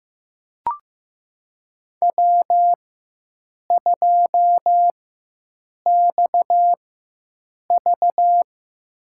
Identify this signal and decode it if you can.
Morse code, heard as a keyed tone, about 15 wpm, spaced as for 7 wpm, 700 Hz. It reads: W2XV
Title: USA Call Signs - 4x Character Spacing - 15wpm